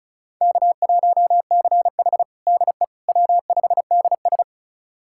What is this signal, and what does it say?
Morse code, heard as a keyed tone, about 36 wpm, 700 Hz. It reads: K1CH DE W5DS
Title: USA Callsign DE Callsign - 4 Characters - 35wpm